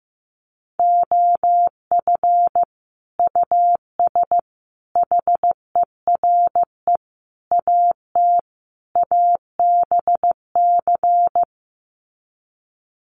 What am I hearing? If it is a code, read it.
Morse code, heard as a keyed tone, about 15 wpm, 700 Hz. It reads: OF US HERE AT ABC
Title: Sets of 5 Words 15wpm